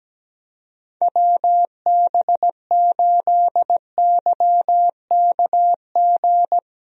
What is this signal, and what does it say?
Morse code, heard as a keyed tone, about 17 wpm, 700 Hz. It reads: WB8YKG
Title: USA Call Signs - 6 Characters - Repeated 2x - 17wpm